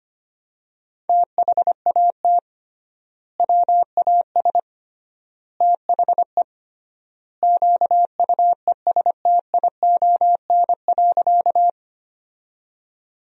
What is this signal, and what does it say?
Morse code, heard as a keyed tone, about 25 wpm, 700 Hz. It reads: THAT WAS THE QUESTION.
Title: 3x Word-Spacing - Sentences from Top 400 Words - 25wpm